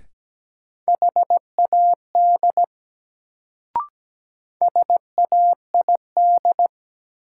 Morse code, heard as a keyed tone, about 17 wpm, 700 Hz. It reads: HAD SAID
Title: Sentences from Top 100 Words - New Word Drill - 17wpm